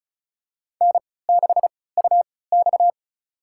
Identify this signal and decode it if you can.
Morse code, heard as a keyed tone, about 36 wpm, spaced as for 20 wpm, 700 Hz. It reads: N6UX
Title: USA Call Signs - 4 Characters - 3x Character Spacing - 35wpm